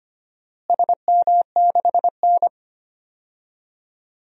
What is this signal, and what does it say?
Morse code, heard as a keyed tone, about 25 wpm, 700 Hz. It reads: SM6N
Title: DXCC-100 Call Signs - 4 Characters - Repeated 3x - 25wpm